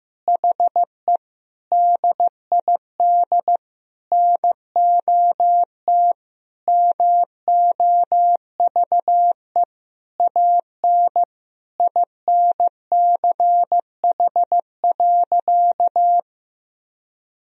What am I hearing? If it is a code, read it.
Morse code, heard as a keyed tone, about 15 wpm, 700 Hz. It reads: HE DID NOT MOVE AN INCH.
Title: Sentences from Top 1000 Words - Review Part 3 - 15wpm